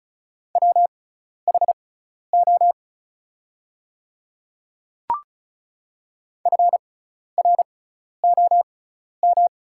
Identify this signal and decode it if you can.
Morse code, heard as a keyed tone, about 36 wpm, spaced as for 12 wpm, 700 Hz. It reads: WHO FROM
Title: 6x Character Spacing - Top 100 Words - 35wpm